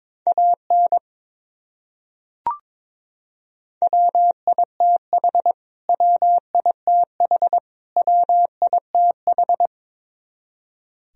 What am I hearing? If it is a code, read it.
Morse code, heard as a keyed tone, about 22 wpm, 700 Hz. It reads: AN WITH WITH WITH
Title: Top 100 Words Repeated 22wpm